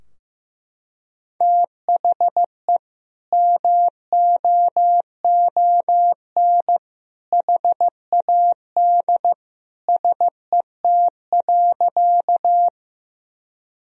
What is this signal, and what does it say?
Morse code, heard as a keyed tone, about 16 wpm, 700 Hz. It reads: THE MOON HAD SET.
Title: Sentences from Top 1000 Words - Review Part 3 - 15wpm